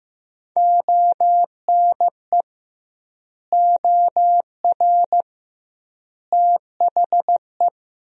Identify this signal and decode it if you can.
Morse code, heard as a keyed tone, about 15 wpm, 700 Hz. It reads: ONE OR THE
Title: 2x Word-Spacing - Sentences from Top 400 Words - 15wpm